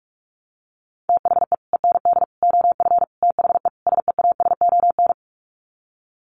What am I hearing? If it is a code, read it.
Morse code, heard as a keyed tone, about 44 wpm, 700 Hz. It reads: THE END OF THE SEASON
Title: Sets of 5 Words Encore 45wpm